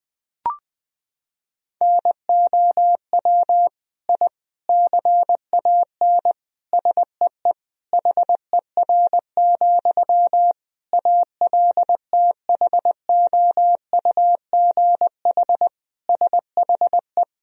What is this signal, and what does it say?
Morse code, heard as a keyed tone, about 20 wpm, 700 Hz. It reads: NOW I CAN SEE HER, ALTHOUGH SHE
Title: Sentences from Top 1000 Words - Review Part 3 - 20wpm